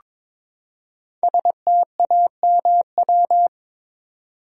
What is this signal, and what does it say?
Morse code, heard as a keyed tone, about 22 wpm, 690 Hz. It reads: STAMW
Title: Code Groups - 5 Letters - 22wpm